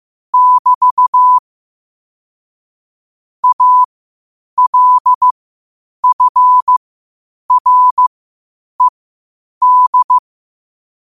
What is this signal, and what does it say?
Morse code, heard as a keyed tone, about 15 wpm, spaced as for 8 wpm, 1000 Hz. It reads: <BT> ALFRED
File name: qotdfiles / 8wpm